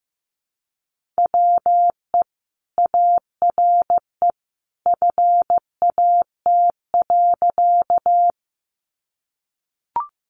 Morse code, heard as a keyed tone, about 14 wpm, 700 Hz. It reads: WE ARE FAT.
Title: Sentences from Top 600 Words 15wpm